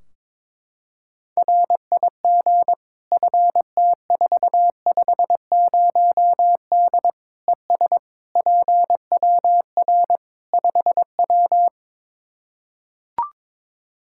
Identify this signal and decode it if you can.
Morse code, heard as a keyed tone, about 22 wpm, 700 Hz. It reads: RIG FT450D ES PWR 5W
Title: Practice QSO - Exchange 2 - Info - 22wpm